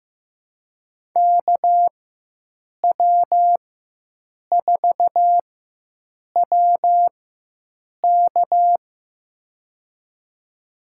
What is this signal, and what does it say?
Morse code, heard as a keyed tone, about 15 wpm, spaced as for 7 wpm, 700 Hz. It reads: KW4WK
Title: USA Call Signs - 5 Characters - 4x Character Spacing - 15wpm